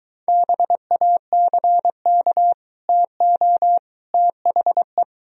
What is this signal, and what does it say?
Morse code, heard as a keyed tone, about 23 wpm, 700 Hz. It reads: BACK TO THE